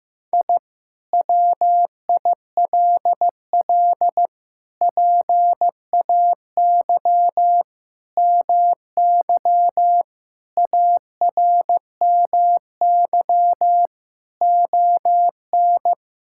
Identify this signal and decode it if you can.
Morse code, heard as a keyed tone, about 15 wpm, 700 Hz. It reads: I WILL PAY MY ARMY ON
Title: Sentences from Top 1000 Words - Review Part 5 - 15wpm